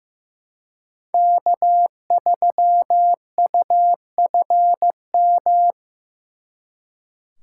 Morse code, heard as a keyed tone, about 18 wpm, 700 Hz. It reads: K3UFM
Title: Speed Racing - USA Call Signs 15wpm